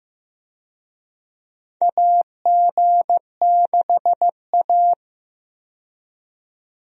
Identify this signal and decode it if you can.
Morse code, heard as a keyed tone, about 15 wpm, 700 Hz. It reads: AG6A
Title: USA Call Signs - 4 Characters - 15wpm